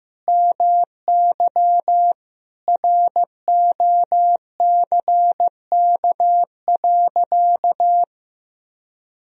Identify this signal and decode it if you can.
Morse code, heard as a keyed tone, about 15 wpm, 700 Hz. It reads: MY ROCK.
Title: Sentences from Top 1000 Words - Review Part 2 - 15wpm